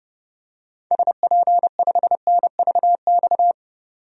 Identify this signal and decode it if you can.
Morse code, heard as a keyed tone, about 31 wpm, 690 Hz. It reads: SP5NVX
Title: DXCC-100 Call Signs - 6 Characters - 30wpm